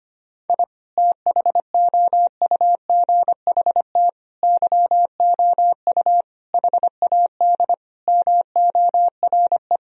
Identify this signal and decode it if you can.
Morse code, heard as a keyed tone, about 25 wpm, 700 Hz. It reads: I THOUGHT YOU HAD MORE
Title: Sentences from Top 1000 Words - Review Part 3 - 25wpm